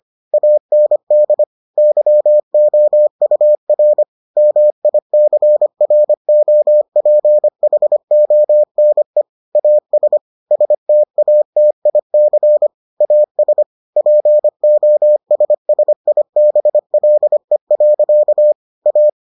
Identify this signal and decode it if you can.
Morse code, heard as a keyed tone, about 25 wpm, 600 Hz. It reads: AND YOUR MICROPHONE AS STATIC AS POSSIBLE. A